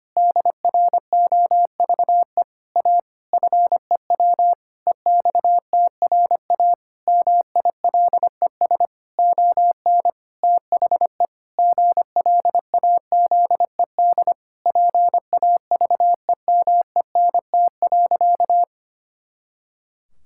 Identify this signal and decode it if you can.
Morse code, heard as a keyed tone, about 25 wpm, 700 Hz. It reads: DROVE A FEW EXTRA MILES ON THE GLAZED PAVEMENT.
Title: Pangram Challenge v2 - 25wpm